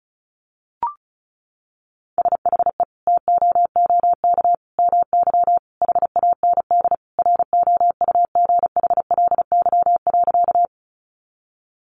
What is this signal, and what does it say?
Morse code, heard as a keyed tone, about 34 wpm, 700 Hz. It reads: SHE TOOK MY HAND ROUGHLY.
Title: Sentences from Top 1500 Words 35wpm